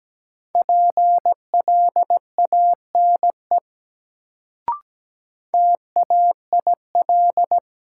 Morse code, heard as a keyed tone, about 17 wpm, 700 Hz. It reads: PLANE TAIL
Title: Sentences from Top 1000 Words - Review Part 3 - New Word Drill - 17wpm